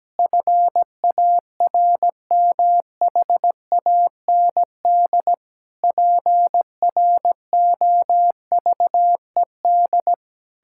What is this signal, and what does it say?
Morse code, heard as a keyed tone, about 17 wpm, 700 Hz. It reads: FARMHAND PROVED
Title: Pangram Challenge v2 - 17wpm